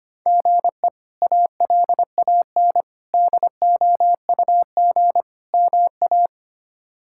Morse code, heard as a keyed tone, about 25 wpm, 700 Hz. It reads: GE ALAN DOUG MA
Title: K1USN SST QSO Practice - 25wpm